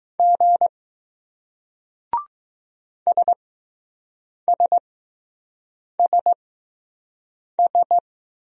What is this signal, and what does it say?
Morse code, heard as a keyed tone, about 20 wpm, spaced as for 6 wpm, 700 Hz. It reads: G SSSS